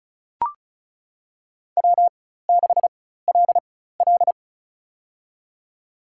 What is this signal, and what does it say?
Morse code, heard as a keyed tone, about 36 wpm, spaced as for 16 wpm, 700 Hz. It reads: W6LL
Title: USA Call Signs - 4 Characters - 4x Character Spacing - 35wpm